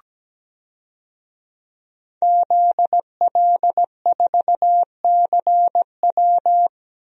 Morse code, heard as a keyed tone, about 17 wpm, 700 Hz. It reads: ZL4CW
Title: DXCC-100 Call Signs - 5 Characters - Repeated 2x - 17wpm